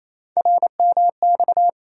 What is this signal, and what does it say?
Morse code, heard as a keyed tone, about 27 wpm, 700 Hz. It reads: RMX
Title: USA Call Sign Postfixes - 3 Letters 28wpm